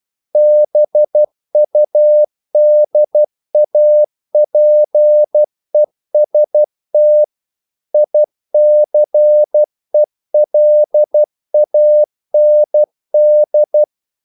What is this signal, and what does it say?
Morse code, heard as a keyed tone, about 12 wpm, 600 Hz. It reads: BUDAPEST ICELAND
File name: CW / Practice / NationalCapitals12